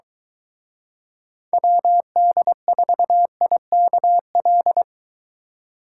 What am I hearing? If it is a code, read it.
Morse code, heard as a keyed tone, about 23 wpm, 700 Hz. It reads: WD4IKL